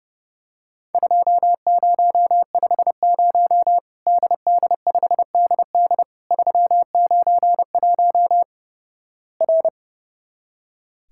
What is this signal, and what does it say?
Morse code, heard as a keyed tone, about 30 wpm, 700 Hz. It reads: 2050 DD5DD 391 R
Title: Worked All Europe DX Contest - QTC Exchange - 30wpm